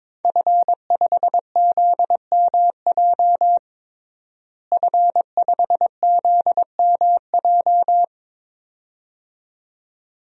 Morse code, heard as a keyed tone, about 22 wpm, 700 Hz. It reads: F5ZMJ F5ZMJ
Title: DXCC-100 Call Signs - 5 Characters - Repeated 2x - 22wpm